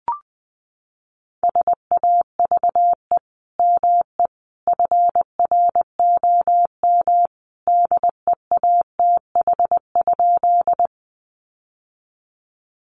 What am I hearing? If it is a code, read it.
Morse code, heard as a keyed tone, about 20 wpm, 700 Hz. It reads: SAVE ME FROM DEATH?
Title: Sentences from Top 1100 Words 20wpm